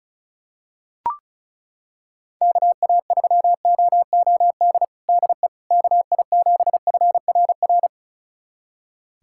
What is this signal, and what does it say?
Morse code, heard as a keyed tone, about 36 wpm, 700 Hz. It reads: KA3OOD DE KI7FRR
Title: USA Callsign DE Callsign 35wpm